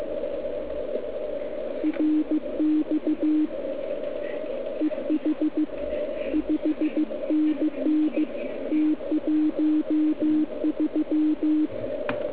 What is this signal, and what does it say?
Morse code, heard as a keyed tone, about 15 wpm, 310 Hz. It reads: RX EH 5NN T13